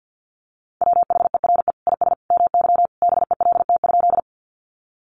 Morse code, heard as a keyed tone, about 50 wpm, 700 Hz. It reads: WHERE IS MY BELT?
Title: Sentences from Top 1400 Words 50wpm